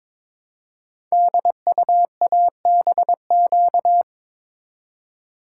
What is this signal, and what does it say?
Morse code, heard as a keyed tone, about 23 wpm, 700 Hz. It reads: DUABQ
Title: Code Groups - 5 Letters - 22wpm